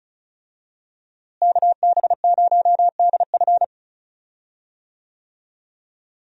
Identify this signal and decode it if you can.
Morse code, heard as a keyed tone, about 36 wpm, 700 Hz. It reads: KB0DF
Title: USA Call Signs - 5 Characters - Repeated 2x - 35wpm